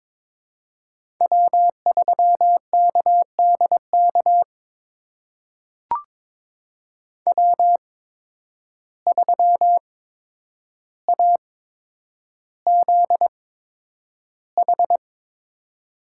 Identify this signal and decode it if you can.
Morse code, heard as a keyed tone, about 22 wpm, spaced as for 9 wpm, 700 Hz. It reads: W3KDK W3AZH
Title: USA Call Signs - 5 Characters - 8x Character Spacing - 22wpm